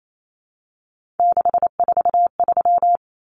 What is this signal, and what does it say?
Morse code, heard as a keyed tone, about 28 wpm, 700 Hz. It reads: 643
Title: Common Cut Numbers - 3-Digit 28wpm